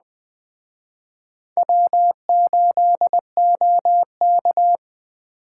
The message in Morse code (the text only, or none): W8OK